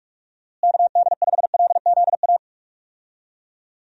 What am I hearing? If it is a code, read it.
Morse code, heard as a keyed tone, about 46 wpm, 700 Hz. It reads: KD5LZA
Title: USA Call Signs - 6 Characters - Repeated 3x - 45wpm